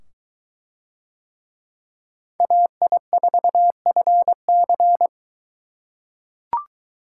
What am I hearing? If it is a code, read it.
Morse code, heard as a keyed tone, about 22 wpm, 700 Hz. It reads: AI4FC